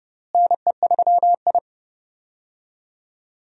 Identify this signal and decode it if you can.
Morse code, heard as a keyed tone, about 31 wpm, 700 Hz. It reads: NE3I
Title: USA Call Signs - 4 Characters - Repeated 3x - 30wpm